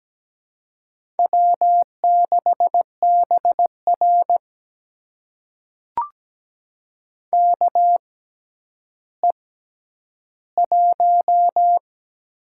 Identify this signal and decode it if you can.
Morse code, heard as a keyed tone, about 17 wpm, spaced as for 10 wpm, 700 Hz. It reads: W6BR KE1
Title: USA Call Signs - 4 Characters - 6x Character Spacing - 17wpm